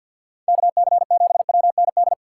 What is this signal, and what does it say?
Morse code, heard as a keyed tone, about 64 wpm, 700 Hz. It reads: KC7WND